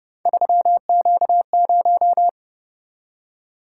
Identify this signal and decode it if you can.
Morse code, heard as a keyed tone, about 28 wpm, 700 Hz. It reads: 3Q0